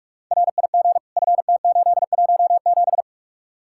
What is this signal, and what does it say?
Morse code, heard as a keyed tone, about 46 wpm, 700 Hz. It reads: RIG FT817